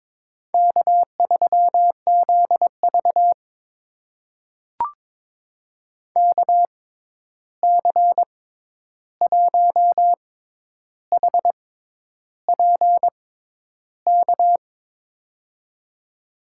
Morse code, heard as a keyed tone, about 22 wpm, spaced as for 10 wpm, 700 Hz. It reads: K3ZV KC1HPK
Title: USA Call Signs - 6x Character Spacing - 22wpm